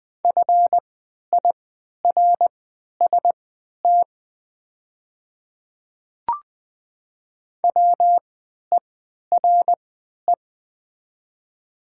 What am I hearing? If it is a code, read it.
Morse code, heard as a keyed tone, about 20 wpm, spaced as for 11 wpm, 700 Hz. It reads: FIRST WERE
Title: 3x Character Spacing - Top 100 Words - 20wpm